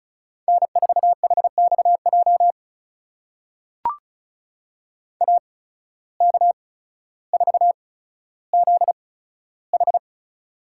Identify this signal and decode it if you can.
Morse code, heard as a keyed tone, about 34 wpm, spaced as for 15 wpm, 700 Hz. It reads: N4HXJ AK4ZH
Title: USA Call Signs - 5 Characters - 8x Character Spacing - 35wpm